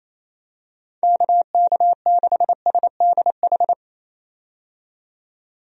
Morse code, heard as a keyed tone, about 28 wpm, 700 Hz. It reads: KK6SDH
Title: USA Call Signs - 6 Characters - 28wpm